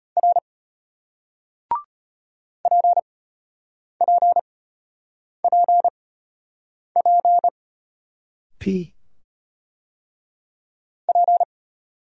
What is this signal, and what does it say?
Morse code, heard as a keyed tone, about 34 wpm, spaced as for 8 wpm, 700 Hz. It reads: R PPPP P